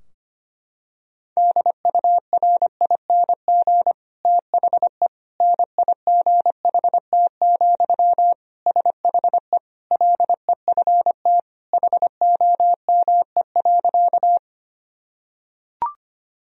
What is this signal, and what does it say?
Morse code, heard as a keyed tone, about 25 wpm, 700 Hz. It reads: DURING THE NIGHT, SHE LEFT HOME.
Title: Sentences from Top 1000 Words - Review Part 2 - 25wpm